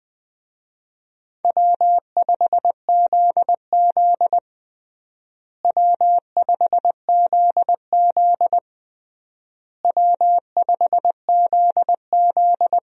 Morse code, heard as a keyed tone, about 20 wpm, 700 Hz. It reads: W5ZZ W5ZZ W5ZZ
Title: USA Call Signs - 4 Characters - Repeated 3x - 20wpm